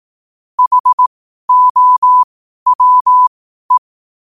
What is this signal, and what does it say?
Morse code, heard as a keyed tone, about 18 wpm, spaced as for 12 wpm, 1000 Hz. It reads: HOWE
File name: qotdfiles / 12wpm